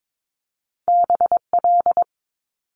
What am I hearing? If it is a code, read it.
Morse code, heard as a keyed tone, about 22 wpm, 700 Hz. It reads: BL
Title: Multi-letter Phonemes Quick Fire 22wpm